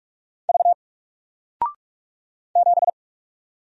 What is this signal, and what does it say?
Morse code, heard as a keyed tone, about 44 wpm, 700 Hz. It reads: V 7
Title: Single Letter-Number Alt 45wpm